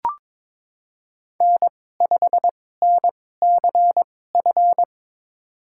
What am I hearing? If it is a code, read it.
Morse code, heard as a keyed tone, about 23 wpm, spaced as for 16 wpm, 700 Hz. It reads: N5NCF
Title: USA Call Signs - 5 Characters - 2x Character Spacing - 22wpm